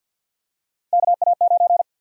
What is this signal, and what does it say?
Morse code, heard as a keyed tone, about 52 wpm, 700 Hz. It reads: KA9
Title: USA Call Sign Prefixes - 50wpm